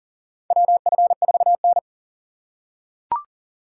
Morse code, heard as a keyed tone, about 40 wpm, 700 Hz. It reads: WF4N
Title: USA Call Signs - 4 Characters - 4x Character Spacing - 40wpm